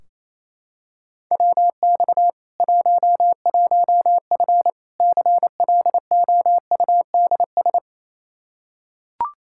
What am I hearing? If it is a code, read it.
Morse code, heard as a keyed tone, about 28 wpm, 700 Hz. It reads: WX 11F CLOUDS
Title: Practice QSO - Exchange 2 - Info - 28wpm